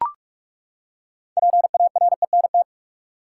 Morse code, heard as a keyed tone, about 42 wpm, 700 Hz. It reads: PARENT
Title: Six Letter Words - 45wpm